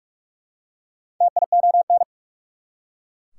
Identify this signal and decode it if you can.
Morse code, heard as a keyed tone, about 43 wpm, 690 Hz. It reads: TION
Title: Prefixes and Suffixes 45wpm